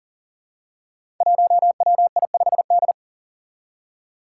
Morse code, heard as a keyed tone, about 41 wpm, 700 Hz. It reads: 1WI5D
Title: Five Challenging Letter-Number Combinations 40wpm